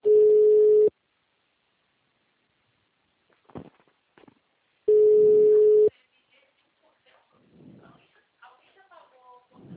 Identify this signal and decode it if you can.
Ringback tone (Europe, 425 Hz), with speech.
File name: out- -2008- - - .27097